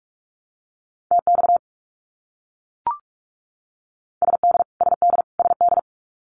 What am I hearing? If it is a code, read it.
Morse code, heard as a keyed tone, about 44 wpm, 700 Hz. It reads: TX SD SD SD
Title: US States Repeated 3x 45wpm